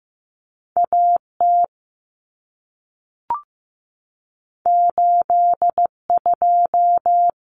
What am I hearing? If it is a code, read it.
Morse code, heard as a keyed tone, about 15 wpm, 700 Hz. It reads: A T 8 2
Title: Common Cut Numbers - 2-Digit 15wpm